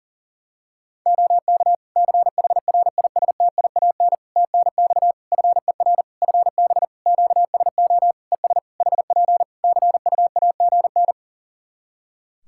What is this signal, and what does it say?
Morse code, heard as a keyed tone, about 40 wpm, 700 Hz. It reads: OK CHRISTIAN TNX FER FB QSO ES HP CUAGN
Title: Practice QSO - Exchange 4 - Ending - 40wpm